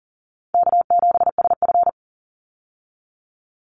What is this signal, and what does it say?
Morse code, heard as a keyed tone, about 40 wpm, 700 Hz. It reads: K7SF
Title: Easy Call Signs Repeated 2x - 40wpm